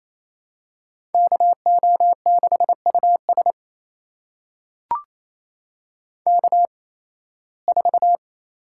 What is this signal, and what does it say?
Morse code, heard as a keyed tone, about 28 wpm, spaced as for 18 wpm, 700 Hz. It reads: KO6US K4
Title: USA Call Signs - 5 Characters - 8x Character Spacing - 28wpm